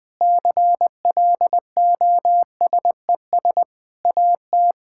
Morse code, heard as a keyed tone, about 20 wpm, 700 Hz. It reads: CLOSES AT
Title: Sentences with 6-Word Phrases - 20wpm